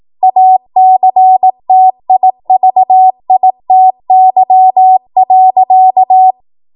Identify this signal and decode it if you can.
Morse code, heard as a keyed tone, about 18 wpm, 750 Hz. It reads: ACTIVITY.
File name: Morse / 18WPM